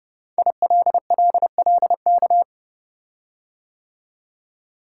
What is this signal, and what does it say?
Morse code, heard as a keyed tone, about 30 wpm, 700 Hz. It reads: ILLLK
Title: Code Groups - 5 Letters - 30wpm